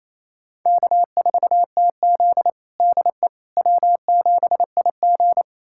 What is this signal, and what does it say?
Morse code, heard as a keyed tone, about 28 wpm, 700 Hz. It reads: K4TZ DE W7IG
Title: USA Callsign DE Callsign - 4 Characters - 28wpm